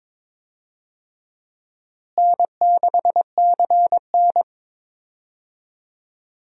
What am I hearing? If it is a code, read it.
Morse code, heard as a keyed tone, about 22 wpm, 700 Hz. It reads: N6CN
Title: USA Call Signs - 4 Characters - Repeated 3x - 22wpm